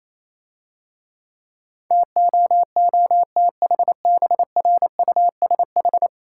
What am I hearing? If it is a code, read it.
Morse code, heard as a keyed tone, about 28 wpm, 700 Hz. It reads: TOOTHBRUSH